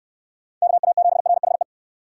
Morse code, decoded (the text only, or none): KA7RHE